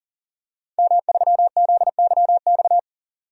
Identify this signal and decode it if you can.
Morse code, heard as a keyed tone, about 40 wpm, 700 Hz. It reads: M3ZYX
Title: DXCC-100 Call Signs - 5 Characters - 40wpm